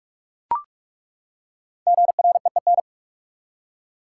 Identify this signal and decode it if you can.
Morse code, heard as a keyed tone, about 43 wpm, 700 Hz. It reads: GREEN
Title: Five Letter Words - 45wpm